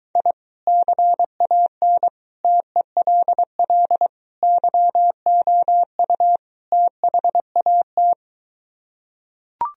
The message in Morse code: I CAN TELL YOU THAT

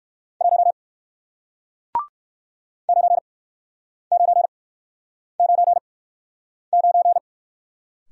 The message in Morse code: 1 9999

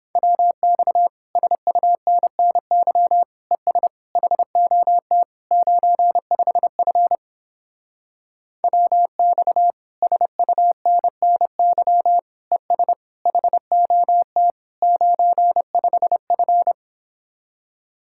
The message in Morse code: WX SUNNY ES HOT 95F WX SUNNY ES HOT 95F

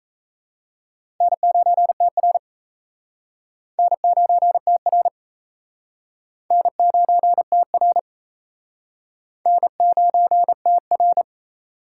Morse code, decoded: N9TR N9TR N9TR N9TR

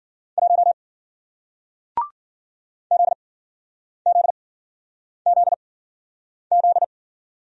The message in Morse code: 1 ZZZZ

